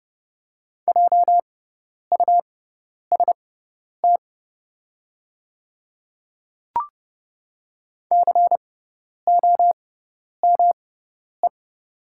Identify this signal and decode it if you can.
Morse code, heard as a keyed tone, about 29 wpm, spaced as for 10 wpm, 700 Hz. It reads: JUST COME
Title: 6x Character Spacing - Top 300 Words - 30wpm